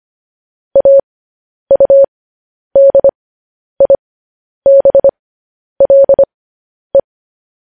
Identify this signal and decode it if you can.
Morse code, heard as a keyed tone, about 25 wpm, spaced as for 10 wpm, 550 Hz. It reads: AUDIBLE
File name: podcast / News headlines-1x-25x10